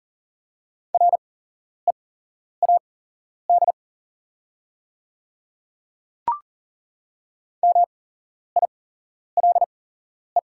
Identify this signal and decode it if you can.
Morse code, heard as a keyed tone, about 40 wpm, spaced as for 11 wpm, 700 Hz. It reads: READ MILE
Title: 8x Character Spacing - Top 300 Words - 40wpm